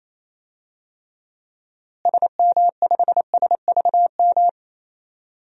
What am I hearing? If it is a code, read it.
Morse code, heard as a keyed tone, about 28 wpm, 700 Hz. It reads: SM5SVM
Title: DXCC-100 Call Signs - 6 Characters - Repeated 2x - 28wpm